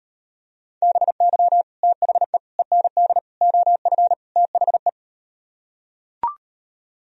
Morse code, heard as a keyed tone, about 38 wpm, 700 Hz. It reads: BY THE END OF THE